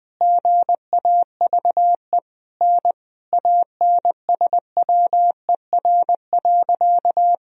Morse code, heard as a keyed tone, about 20 wpm, 700 Hz. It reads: GAVE N ANSWER.
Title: Sentences from Top 500 Words and Abbreviations 20wpm